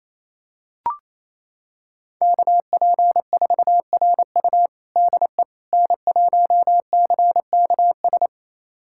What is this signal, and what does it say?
Morse code, heard as a keyed tone, about 28 wpm, 700 Hz. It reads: KP4RU DE N1CKS
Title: USA Callsign DE Callsign - 5 Characters - 28wpm